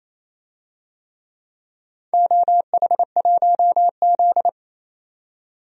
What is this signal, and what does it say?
Morse code, heard as a keyed tone, about 28 wpm, 700 Hz. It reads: OH1Z
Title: DXCC-100 Call Signs - 4 Characters - 28wpm